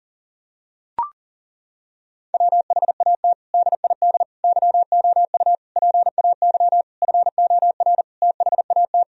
Morse code, heard as a keyed tone, about 37 wpm, 700 Hz. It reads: WHAT DID YOU PAY FOR THAT
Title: Sentences from Top 1000 Words - Review Part 3 - 40wpm